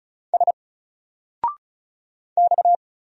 Morse code, heard as a keyed tone, about 33 wpm, 700 Hz. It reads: S X
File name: Single-Letter-Number-35wpm